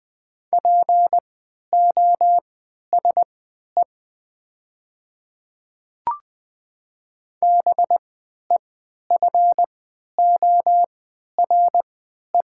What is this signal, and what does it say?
Morse code, heard as a keyed tone, about 20 wpm, spaced as for 11 wpm, 700 Hz. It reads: POSE BEFORE
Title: 3x Character Spacing - Top 500 Words - 20wpm